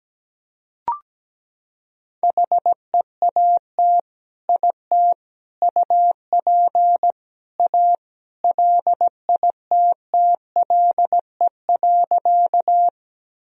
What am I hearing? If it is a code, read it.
Morse code, heard as a keyed tone, about 17 wpm, 700 Hz. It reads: HEAT IT UP A LITTLE.
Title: Sentences from Top 1000 Words - Review Part 3 - 17wpm